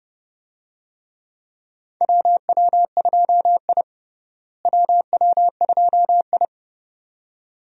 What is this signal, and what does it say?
Morse code, heard as a keyed tone, about 30 wpm, 700 Hz. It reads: WW2I WW2I
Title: USA Call Signs - 4 Characters - Repeated 2x - 30wpm